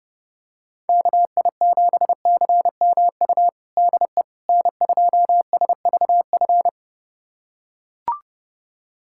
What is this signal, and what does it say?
Morse code, heard as a keyed tone, about 30 wpm, 700 Hz. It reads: KI7CMU DE N2SVF
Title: USA Callsign DE Callsign 30wpm